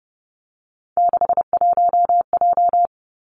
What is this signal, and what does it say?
Morse code, heard as a keyed tone, about 30 wpm, 700 Hz. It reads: 61J
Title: Three Challenging Letter-Number Combinations 30wpm